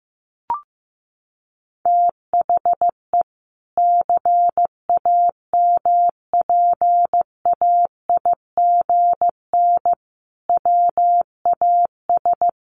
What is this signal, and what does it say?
Morse code, heard as a keyed tone, about 15 wpm, 700 Hz. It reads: THE CAMPAIGN WAS
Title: Sentences from Top 1400 Words 15wpm